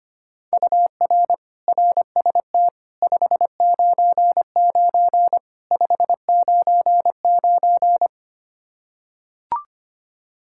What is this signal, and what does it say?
Morse code, heard as a keyed tone, about 25 wpm, 700 Hz. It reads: UR RST 599 599
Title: K1USN SST QSO Practice 25wpm